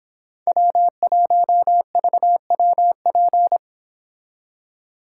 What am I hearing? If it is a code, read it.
Morse code, heard as a keyed tone, about 26 wpm, 700 Hz. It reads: W1VWP